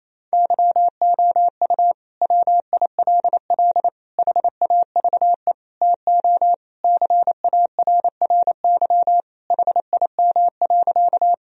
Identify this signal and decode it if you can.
Morse code, heard as a keyed tone, about 28 wpm, 700 Hz. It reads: YOU WILL HAVE TO CARRY HIM.
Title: Sentences from Top 1000 Words - Review Part 1 - 28wpm